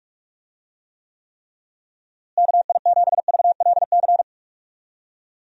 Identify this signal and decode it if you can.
Morse code, heard as a keyed tone, about 44 wpm, 700 Hz. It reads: KI7VLC
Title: USA Call Signs - 6 Characters - Repeated 3x - 45wpm